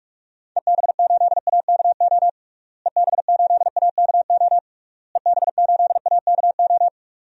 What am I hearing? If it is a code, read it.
Morse code, heard as a keyed tone, about 41 wpm, 700 Hz. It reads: EB8AKO EB8AKO EB8AKO
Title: DXCC-100 Call Signs - 6 Characters - Repeated 3x - 45wpm